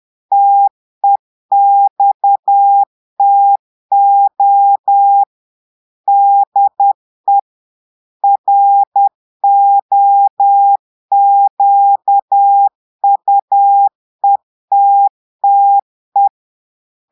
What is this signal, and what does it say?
Morse code, heard as a keyed tone, about 10 wpm, 800 Hz. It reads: TEXTO DE ROQUETTE
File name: Portugais - texte en clair 1 - vitesse - 10 mots minute